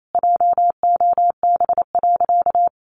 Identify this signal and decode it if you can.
Morse code, heard as a keyed tone, about 28 wpm, 700 Hz. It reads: JOB.
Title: Sentences from Top 700 Words 28wpm